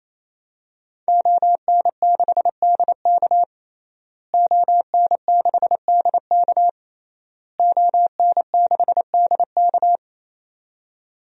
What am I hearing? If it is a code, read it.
Morse code, heard as a keyed tone, about 27 wpm, 700 Hz. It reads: ON6DK ON6DK ON6DK
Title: DXCC-100 Call Signs - 5 Characters - Repeated 3x - 28wpm